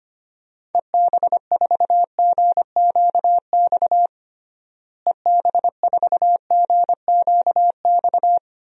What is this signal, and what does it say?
Morse code, heard as a keyed tone, about 25 wpm, 700 Hz. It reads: EB4GQX EB4GQX
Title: DXCC-100 Call Signs - 6 Characters - Repeated 2x - 25wpm